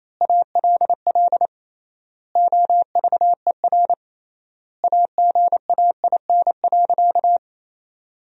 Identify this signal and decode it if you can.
Morse code, heard as a keyed tone, about 27 wpm, 700 Hz. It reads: ALL OVER AGAIN.
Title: 3x Word-Spacing - Sentences from Top 500 Words - 28wpm